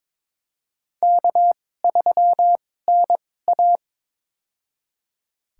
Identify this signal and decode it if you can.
Morse code, heard as a keyed tone, about 21 wpm, spaced as for 16 wpm, 700 Hz. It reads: K3NA
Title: USA Call Signs - 4 Characters - 2x Character Spacing - 22wpm